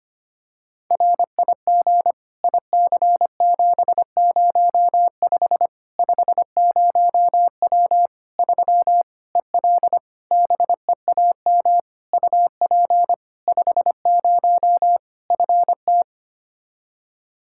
Morse code, heard as a keyed tone, about 25 wpm, 700 Hz. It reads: RIG IC705 50W 3 EL BEAM UP 50 FT
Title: Practice QSO - Exchange 2 - Info - 25wpm